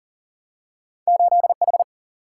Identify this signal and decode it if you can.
Morse code, heard as a keyed tone, about 39 wpm, 700 Hz. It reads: 8H
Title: Single-Letter Single-Number 40wpm